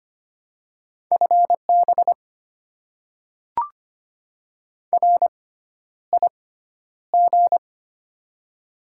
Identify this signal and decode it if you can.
Morse code, heard as a keyed tone, about 25 wpm, spaced as for 11 wpm, 690 Hz. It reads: FB RIG
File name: Top 20 QSO Elements - 6x Character Spacing - 25wpm